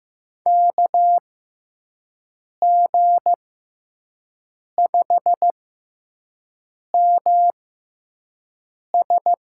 Morse code, heard as a keyed tone, about 15 wpm, spaced as for 5 wpm, 700 Hz. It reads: KG5MS
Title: USA Call Signs - 6 Characters - 6x Character Spacing - 15wpm